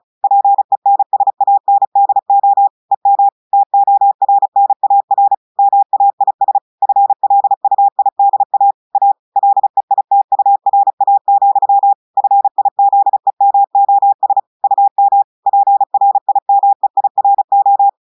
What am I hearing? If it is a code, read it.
Morse code, heard as a keyed tone, about 36 wpm, 800 Hz. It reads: PENSANDO EM TORNAR MAIS FLUIDA A LEITURA, FIZEMOS UM PRIMEIRO
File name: audios / portugais / portugais texte en clair 1 vitesse 35 mots minutes